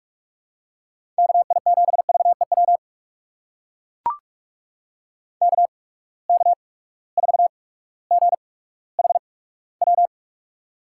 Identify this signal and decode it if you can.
Morse code, heard as a keyed tone, about 44 wpm, spaced as for 19 wpm, 700 Hz. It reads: KI7VEW KK4GHW
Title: USA Call Signs - 6 Characters - 8x Character Spacing - 45wpm